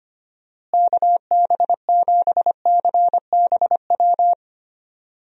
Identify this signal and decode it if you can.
Morse code, heard as a keyed tone, about 25 wpm, 700 Hz. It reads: KB7CBW
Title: USA Call Signs - 6 Characters - Repeated 3x - 25wpm